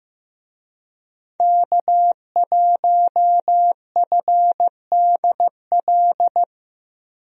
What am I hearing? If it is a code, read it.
Morse code, heard as a keyed tone, about 15 wpm, 700 Hz. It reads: K1FDL
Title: USA Call Signs - 5 Characters - Repeated 2x - 15wpm